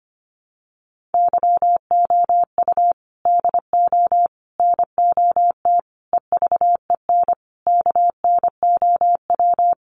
Morse code, heard as a keyed tone, about 25 wpm, 700 Hz. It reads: YOU DO NOT EVEN KNOW
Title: Sets of 5 Words Encore2 25wpm